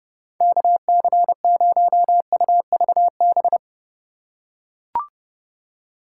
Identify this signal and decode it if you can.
Morse code, heard as a keyed tone, about 30 wpm, 700 Hz. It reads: KC0UVB